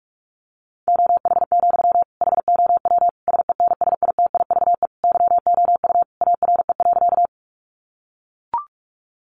Morse code, heard as a keyed tone, about 44 wpm, 700 Hz. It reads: OH, HOW SENSITIVE YOU ARE.
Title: Sentences from Top 1400 Words 45wpm